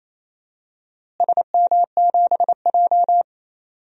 Morse code, heard as a keyed tone, about 28 wpm, 700 Hz. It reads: SM7J
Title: DXCC-100 Call Signs - 4 Characters - Repeated 3x - 28wpm